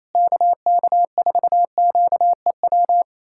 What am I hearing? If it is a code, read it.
Morse code, heard as a keyed tone, about 28 wpm, 700 Hz. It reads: KK4QEW
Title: USA Callsign DE Callsign 28wpm